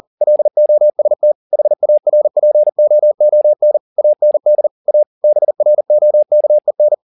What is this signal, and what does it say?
Morse code, heard as a keyed tone, about 40 wpm, 600 Hz. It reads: LOST HARPOON AND A BROKEN
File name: the old man and the sea 2 40wpm